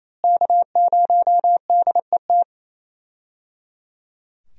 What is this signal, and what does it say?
Morse code, heard as a keyed tone, about 27 wpm, 700 Hz. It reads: K0DET
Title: USA Call Signs - 5 Characters - Repeated 3x - 28wpm